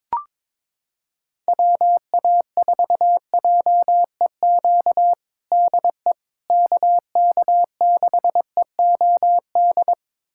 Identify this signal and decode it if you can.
Morse code, heard as a keyed tone, about 22 wpm, 700 Hz. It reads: WA4JEQ DE KK6EOD
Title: USA Callsign DE Callsign 22wpm